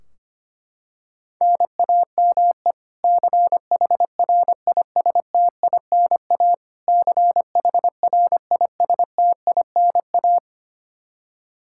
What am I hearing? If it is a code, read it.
Morse code, heard as a keyed tone, about 25 wpm, 700 Hz. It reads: NAME CHRISTINA CHRISTINA
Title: K1USN SST QSO Practice 25wpm